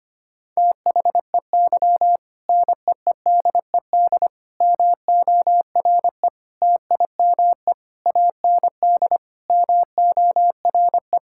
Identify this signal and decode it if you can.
Morse code, heard as a keyed tone, about 25 wpm, 700 Hz. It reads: THEY NEEDED MORE TIME AND MORE
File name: practice / 25wpm / Sentences with 2-Word Phrases - 25wpm